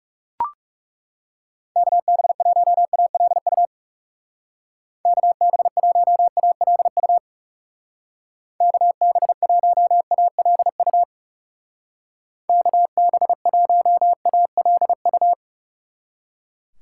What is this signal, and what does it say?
Morse code, heard as a keyed tone, about 39 wpm, 700 Hz. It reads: KB1ALU KB1ALU KB1ALU KB1ALU